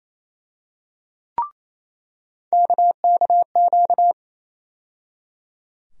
Morse code, heard as a keyed tone, about 28 wpm, 700 Hz. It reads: KKQ
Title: USA Call Sign Postfixes - 3 Letters 28wpm